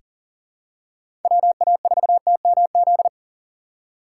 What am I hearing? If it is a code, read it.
Morse code, heard as a keyed tone, about 40 wpm, 700 Hz. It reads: WA4TMZ
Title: USA Call Signs - 3x Character Spacing - 40wpm